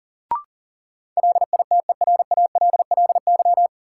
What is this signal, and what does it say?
Morse code, heard as a keyed tone, about 40 wpm, 700 Hz. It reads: LITERALLY
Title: Sentences from Top 1400 Words - New Word Drill Alt - 40wpm